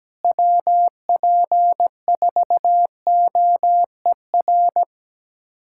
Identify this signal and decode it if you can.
Morse code, heard as a keyed tone, about 17 wpm, 700 Hz. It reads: WP4OER
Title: USA Callsign DE Callsign - 6 Characters - 17wpm